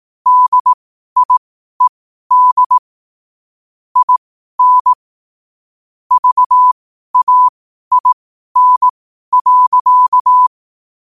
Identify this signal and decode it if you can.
Morse code, heard as a keyed tone, about 18 wpm, spaced as for 12 wpm, 1000 Hz. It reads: DIED IN VAIN.
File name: qotdfiles / 12wpm